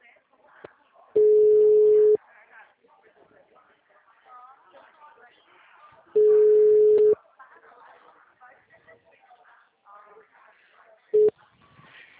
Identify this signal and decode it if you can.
Ringback tone (Europe, 425 Hz), with speech.
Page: Bebedouro 3